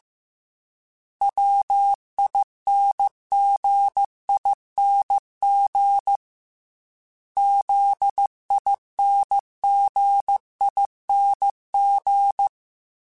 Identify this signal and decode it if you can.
Morse code, heard as a keyed tone, about 15 wpm, 800 Hz. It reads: WINGING ZINGING
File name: Special INGs / Special INGs-mp3s / Dble Ings @13